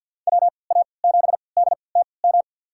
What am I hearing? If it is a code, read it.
Morse code, heard as a keyed tone, about 48 wpm, spaced as for 28 wpm, 700 Hz. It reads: WA7DTM
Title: USA Call Signs - 6 Characters - 3x Character Spacing - 50wpm